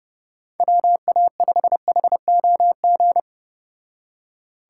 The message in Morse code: WA5HOG

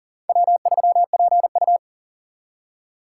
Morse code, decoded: W3PU

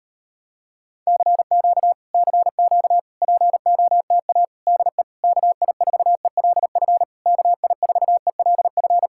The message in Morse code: CQ CQ POTA DE KI4ELF KI4ELF

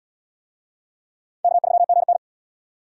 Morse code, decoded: K3GN

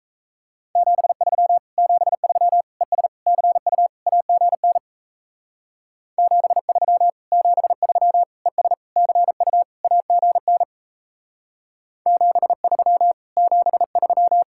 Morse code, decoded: 73 73 ES CU AGN 73 73 ES CU AGN 73 73

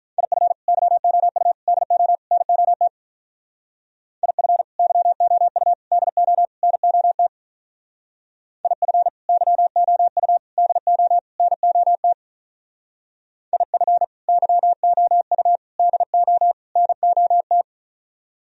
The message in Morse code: IF YOU DO NOT IF YOU DO NOT IF YOU DO NOT IF YOU DO NOT

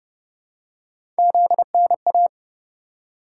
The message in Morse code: ZNA

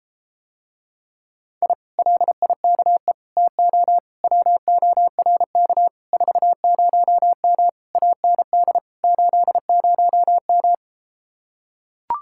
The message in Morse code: I LIKE TO WORK 40M AND 80M

